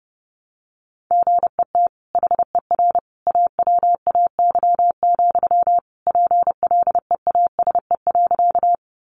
Morse code, heard as a keyed tone, about 30 wpm, 700 Hz. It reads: GET HER AWAY, PLEASE.
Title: Sentences from Top 1500 Words 30wpm